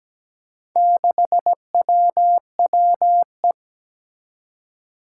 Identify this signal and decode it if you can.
Morse code, heard as a keyed tone, about 17 wpm, 700 Hz. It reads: 6WWE
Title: Four Challenging Letter-Number Combinations 17wpm